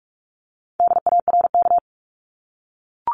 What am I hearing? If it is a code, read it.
Morse code, heard as a keyed tone, about 44 wpm, 700 Hz. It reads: DARK
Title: Top 500 Words 45wpm